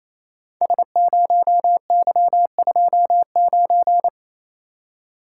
Morse code, heard as a keyed tone, about 29 wpm, 700 Hz. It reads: S0Y29
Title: Code Groups - 5 Letter:Numbers - 28wpm